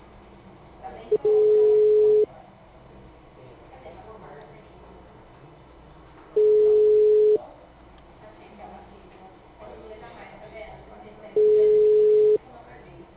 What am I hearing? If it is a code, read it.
Ringback tone (Europe, 425 Hz), with speech.
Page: Ribeirao sala 906